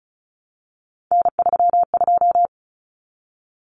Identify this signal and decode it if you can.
Morse code, heard as a keyed tone, about 34 wpm, 700 Hz. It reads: N32
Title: Common Cut Numbers - 3-Digit 35wpm